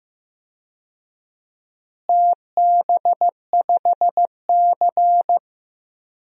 Morse code, heard as a keyed tone, about 16 wpm, 700 Hz. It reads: TB5C
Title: DXCC-100 Call Signs - 4 Characters - Repeated 2x - 15wpm